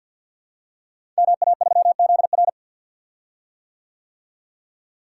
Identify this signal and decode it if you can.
Morse code, heard as a keyed tone, about 52 wpm, 700 Hz. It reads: MA3ZR
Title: Code Groups - 5 Letter:Numbers - 50wpm